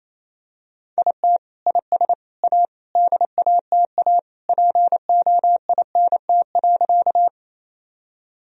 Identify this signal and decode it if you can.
Morse code, heard as a keyed tone, about 28 wpm, 700 Hz. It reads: IT IS A DATA POINT.
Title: Sentences from Top 1000 Words - Review Part 3 - 28wpm